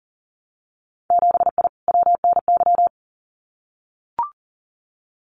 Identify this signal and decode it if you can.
Morse code, heard as a keyed tone, about 40 wpm, 700 Hz. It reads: 7I WNY
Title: Winter Field Day Exchanges Encore 40wpm